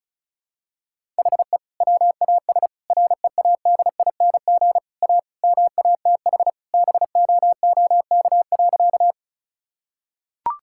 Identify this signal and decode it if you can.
Morse code, heard as a keyed tone, about 34 wpm, 700 Hz. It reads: HE WAS READING A MATH BOOK.
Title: Sentences from Top 1000 Words - Review Part 6 - 35wpm